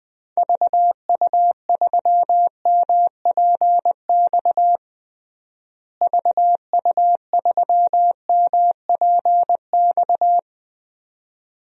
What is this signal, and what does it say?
Morse code, heard as a keyed tone, about 20 wpm, 700 Hz. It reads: VU3MPX VU3MPX
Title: DXCC-100 Call Signs - 6 Characters - Repeated 2x - 20wpm